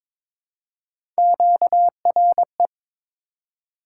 Morse code, heard as a keyed tone, about 22 wpm, 700 Hz. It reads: QRE
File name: practice / 22wpm / Three Letter-Number Combinations - 22wpm